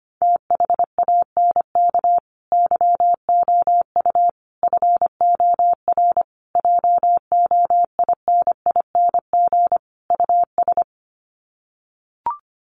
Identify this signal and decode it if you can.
Morse code, heard as a keyed tone, about 25 wpm, 700 Hz. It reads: THANK YOU FOR JOINING US
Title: Sets of 5 Words 25wpm